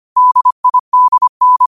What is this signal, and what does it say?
Morse code, heard as a keyed tone, about 25 wpm, 1000 Hz. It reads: DIDN
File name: qotdfiles / 25wpm